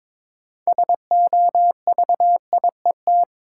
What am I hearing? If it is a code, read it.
Morse code, heard as a keyed tone, about 22 wpm, 700 Hz. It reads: SOVIET
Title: Sets of 5 Words Encore - New Word Drill - 22wpm